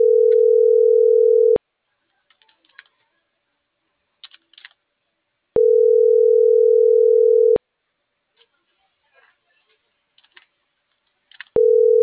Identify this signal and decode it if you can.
Ringback tone (North America), 440 + 480 Hz.